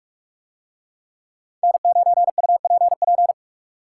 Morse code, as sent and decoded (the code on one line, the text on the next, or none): -. ----. ..- .--. .--.
N9UPP